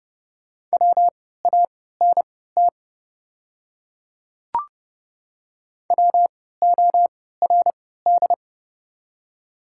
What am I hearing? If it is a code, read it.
Morse code, heard as a keyed tone, about 32 wpm, spaced as for 17 wpm, 700 Hz. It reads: WANT WORD
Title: 3x Character Spacing - Top 200 Words - 30wpm